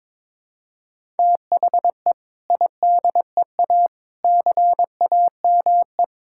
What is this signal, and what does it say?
Morse code, heard as a keyed tone, about 22 wpm, 700 Hz. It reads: THE IDEA CAME
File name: practice / 22wpm / Sentences with 2-Word Phrases - 22wpm